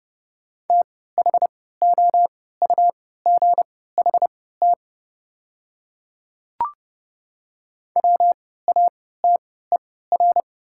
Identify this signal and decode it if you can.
Morse code, heard as a keyed tone, about 30 wpm, spaced as for 17 wpm, 700 Hz. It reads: THOUGHT WATER
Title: 3x Character Spacing - Top 300 Words - 30wpm